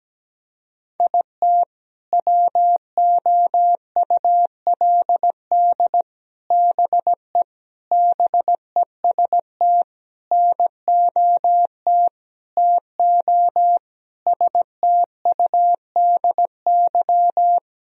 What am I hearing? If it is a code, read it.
Morse code, heard as a keyed tone, about 17 wpm, 700 Hz. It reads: IT WOULD BE BEST NOT TO STUDY
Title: Sentences from Top 1000 Words - Review Part 1 - 17wpm